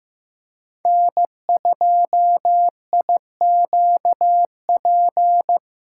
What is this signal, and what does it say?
Morse code, heard as a keyed tone, about 15 wpm, 700 Hz. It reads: N2IQP
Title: USA Callsign DE Callsign - 5 Characters - 15wpm